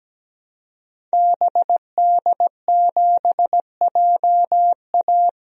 Morse code, heard as a keyed tone, about 17 wpm, 700 Hz. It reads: BD7JA
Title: DXCC-100 Call Signs - 5 Characters - 17wpm